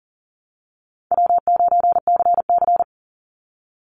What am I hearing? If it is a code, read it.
Morse code, heard as a keyed tone, about 40 wpm, 700 Hz. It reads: W9CC
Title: Easy Call Signs Repeated 2x - 40wpm